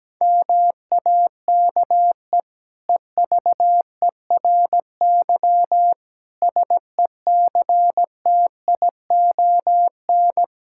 Morse code, heard as a keyed tone, about 17 wpm, 700 Hz. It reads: MAKE EVERY SECTION
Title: Sentences from Top 1000 Words - Review Part 5 - 17wpm